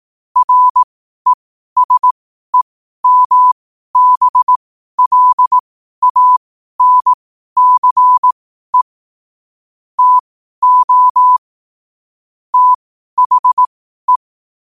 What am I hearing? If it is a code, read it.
Morse code, heard as a keyed tone, about 18 wpm, spaced as for 12 wpm, 1000 Hz. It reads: RESEMBLANCE TO THE
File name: qotdfiles / 12wpm